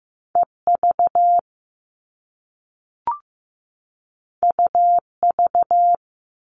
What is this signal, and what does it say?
Morse code, heard as a keyed tone, about 15 wpm, 700 Hz. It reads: EV UV
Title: All Cut Numbers - 2-Digit 15wpm